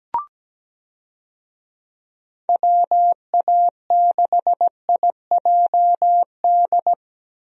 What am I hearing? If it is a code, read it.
Morse code, heard as a keyed tone, about 17 wpm, 700 Hz. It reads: WA6IJD
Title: USA Call Signs - 6 Characters - Repeated 3x - 17wpm